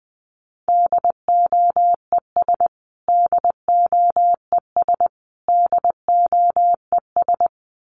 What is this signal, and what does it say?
Morse code, heard as a keyed tone, about 20 wpm, 700 Hz. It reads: DOES DOES DOES
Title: Top 300 Words Repeated 20wpm